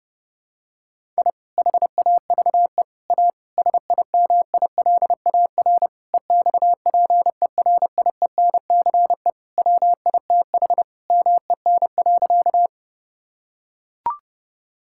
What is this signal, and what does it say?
Morse code, heard as a keyed tone, about 30 wpm, 700 Hz. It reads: I HAVE A SIMILAR EXPERIENCE WITH MEN.
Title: Sentences from Top 1000 Words - Review Part 5 - 30wpm